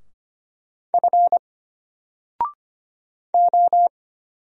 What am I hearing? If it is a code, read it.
Morse code, heard as a keyed tone, about 25 wpm, 700 Hz. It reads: F O
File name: Single-Letter-Number-25wpm